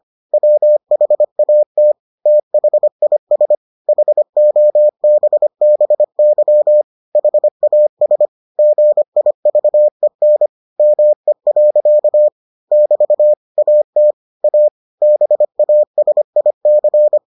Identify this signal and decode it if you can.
Morse code, heard as a keyed tone, about 25 wpm, 600 Hz. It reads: WHAT THIS HOBBY HAS GIVEN ME. <BT> AT A BASIC